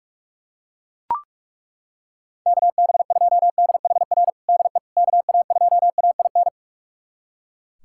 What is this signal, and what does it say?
Morse code, heard as a keyed tone, about 42 wpm, 700 Hz. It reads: KB2DHR DE KA2AIN
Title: USA Callsign DE Callsign - 6 Characters - 45wpm